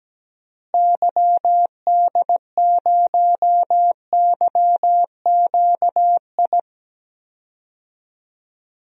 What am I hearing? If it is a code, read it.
Morse code, heard as a keyed tone, about 17 wpm, 700 Hz. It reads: YD0YQI
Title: DXCC-100 Call Signs - 6 Characters - Repeated 2x - 17wpm